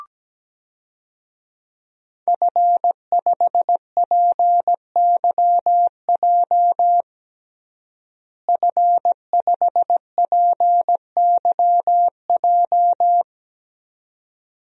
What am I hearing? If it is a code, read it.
Morse code, heard as a keyed tone, about 17 wpm, 700 Hz. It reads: F5PYJ F5PYJ
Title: DXCC-100 Call Signs - 5 Characters - Repeated 2x - 17wpm